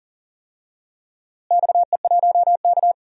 Morse code, heard as a keyed tone, about 40 wpm, 700 Hz. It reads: XE1K
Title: DXCC-100 Call Signs - 4 Characters - Repeated 2x - 40wpm